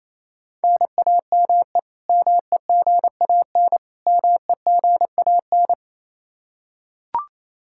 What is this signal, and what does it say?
Morse code, heard as a keyed tone, about 28 wpm, 700 Hz. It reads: NAME MEGAN MEGAN
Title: Practice QSO - Exchange 1 - Beginning - 28wpm